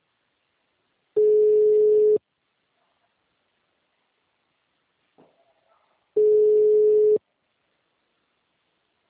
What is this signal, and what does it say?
Ringback tone (Europe, 425 Hz).
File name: out- -2010